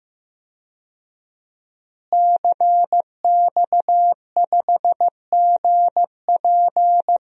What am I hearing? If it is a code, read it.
Morse code, heard as a keyed tone, about 16 wpm, 700 Hz. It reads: CX5GP
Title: DXCC-100 Call Signs - 5 Characters - Repeated 2x - 15wpm